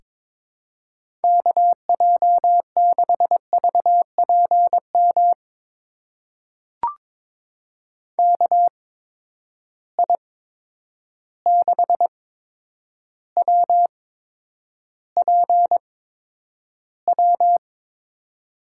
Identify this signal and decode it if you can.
Morse code, heard as a keyed tone, about 22 wpm, spaced as for 9 wpm, 700 Hz. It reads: KJ6VPM KI6WPW
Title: USA Call Signs - 6 Characters - 8x Character Spacing - 22wpm